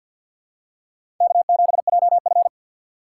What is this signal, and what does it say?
Morse code, heard as a keyed tone, about 53 wpm, 700 Hz. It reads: K7JF
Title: USA Call Signs - 4 Characters - 3x Character Spacing - 50wpm